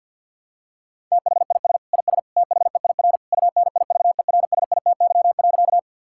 Morse code, heard as a keyed tone, about 50 wpm, 700 Hz. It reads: THIS IS THEIR UNIVERSITY.
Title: Sentences from Top 1000 Words - Review Part 5 - 50wpm